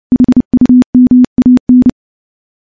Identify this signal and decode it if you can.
Morse code, heard as a keyed tone, about 30 wpm, 250 Hz. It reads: HUMAN
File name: morse / morse-training-30WPM